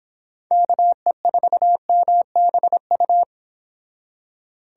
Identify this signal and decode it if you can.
Morse code, heard as a keyed tone, about 29 wpm, 700 Hz. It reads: KE4MBU